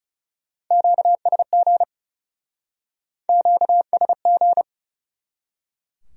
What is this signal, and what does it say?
Morse code, heard as a keyed tone, about 37 wpm, 700 Hz. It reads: QSG QSG
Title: Speed-Racing Q-Codes - 30wpm